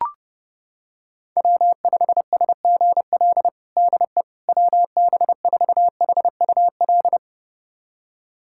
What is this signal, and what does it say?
Morse code, heard as a keyed tone, about 30 wpm, 700 Hz. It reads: W5SGL DE WB4HUL
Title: USA Callsign DE Callsign 30wpm